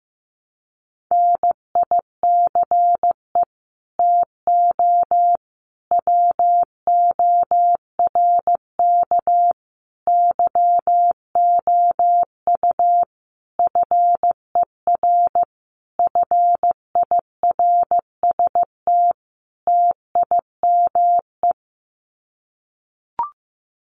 Morse code, heard as a keyed tone, about 15 wpm, 700 Hz. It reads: NICE TO WORK YOU FER FIRST TIME
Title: Common QSO Phrases 15wpm